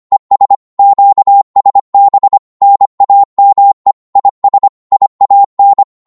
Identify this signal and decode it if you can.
Morse code, heard as a keyed tone, about 25 wpm, 800 Hz. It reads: ES QSB NAME IS IAN
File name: Morse / qso-25wpm